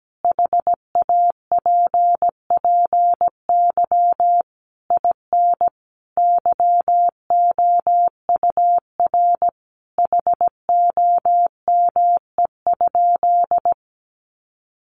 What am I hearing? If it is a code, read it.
Morse code, heard as a keyed tone, about 17 wpm, 700 Hz. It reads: HAPPY IN YOUR HOME?
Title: Sentences from Top 1000 Words 17wpm